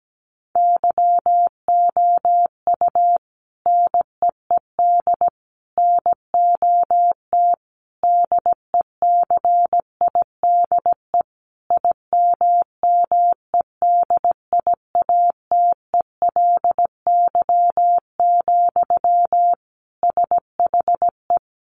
Morse code, heard as a keyed tone, about 17 wpm, 700 Hz. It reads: YOU NEED NOT DECIDE IMMEDIATELY, SHE
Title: Sentences from Top 1500 Words 17wpm